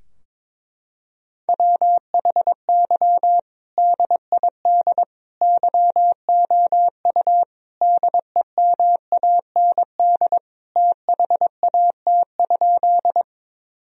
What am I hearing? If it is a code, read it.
Morse code, heard as a keyed tone, about 22 wpm, 700 Hz. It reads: WHY DID YOU DEMAND THAT?
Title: Sentences from Top 1000 Words - Review Part 4 - 22wpm